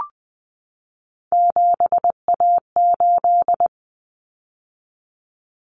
Morse code, heard as a keyed tone, about 20 wpm, 700 Hz. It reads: 7A8
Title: Common Cut Numbers - 3-Digit 20wpm